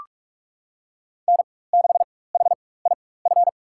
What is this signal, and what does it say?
Morse code, heard as a keyed tone, about 42 wpm, spaced as for 20 wpm, 690 Hz. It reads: N6HIF
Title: USA Call Signs - 5 Characters - 4x Character Spacing - 45wpm